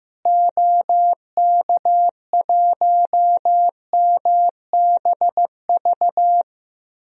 Morse code, heard as a keyed tone, about 15 wpm, 700 Hz. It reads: OK1MBV
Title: DXCC-100 Call Signs - 6 Characters - Repeated 3x - 15wpm